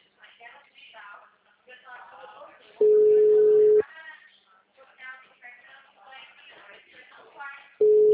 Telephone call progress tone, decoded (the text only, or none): ringback tone